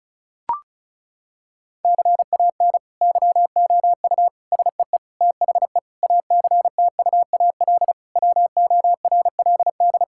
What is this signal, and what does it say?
Morse code, heard as a keyed tone, about 34 wpm, 700 Hz. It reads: CAN YOU SEE THE ACTUAL WORLD
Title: Sentences from Top 1000 Words - Review Part 5 - 35wpm